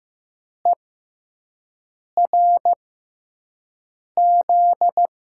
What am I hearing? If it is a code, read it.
Morse code, heard as a keyed tone, about 15 wpm, spaced as for 5 wpm, 700 Hz. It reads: ERZ
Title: USA Call Signs - 5 Characters - 6x Character Spacing - 15wpm